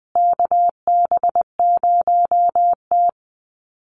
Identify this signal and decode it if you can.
Morse code, heard as a keyed tone, about 20 wpm, 700 Hz. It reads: KB0T
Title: USA Call Signs Repeated 3x 20wpm